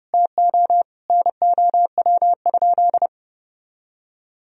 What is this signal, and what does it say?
Morse code, heard as a keyed tone, about 30 wpm, 700 Hz. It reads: TO NOW?
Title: Sentences from Top 200 Words and Abbreviations 30wpm